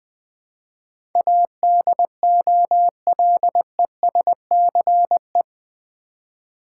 Morse code, heard as a keyed tone, about 20 wpm, 700 Hz. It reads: ADOLESCE
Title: Build A Word - 4 Letters At A Time - 10-Letter Words - 20wpm